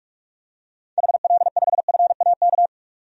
Morse code, heard as a keyed tone, about 47 wpm, 700 Hz. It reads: HL5FAK
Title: DXCC-100 Call Signs - 6 Characters - 45wpm